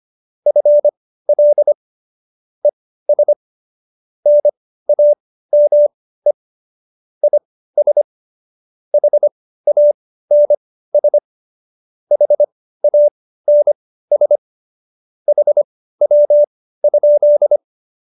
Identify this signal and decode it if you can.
Morse code, heard as a keyed tone, about 25 wpm, spaced as for 15 wpm, 590 Hz. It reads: FL ES NAME IS HANS HANS HW?
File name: qso222 15